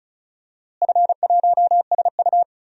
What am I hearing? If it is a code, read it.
Morse code, heard as a keyed tone, about 34 wpm, 700 Hz. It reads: F1SU
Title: DXCC-100 Call Signs - 4 Characters - Repeated 2x - 35wpm